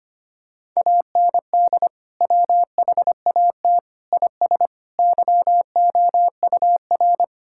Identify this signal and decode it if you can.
Morse code, heard as a keyed tone, about 25 wpm, 700 Hz. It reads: AND WHAT IS YOUR
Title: Sentences from Top 1000 Words - Review Part 6 - 25wpm